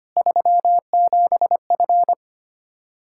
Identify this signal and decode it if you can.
Morse code, heard as a keyed tone, about 25 wpm, 700 Hz. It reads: 37F
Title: K1USN SST QSO Practice 25wpm